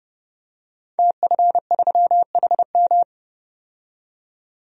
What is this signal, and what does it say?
Morse code, heard as a keyed tone, about 30 wpm, 700 Hz. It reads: TF3HM
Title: DXCC-100 Call Signs - 5 Characters - 30wpm